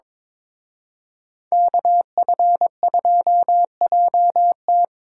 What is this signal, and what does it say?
Morse code, heard as a keyed tone, about 26 wpm, 700 Hz. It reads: KF2JT
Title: Speed Racing - USA 2x2 Call Signs 22wpm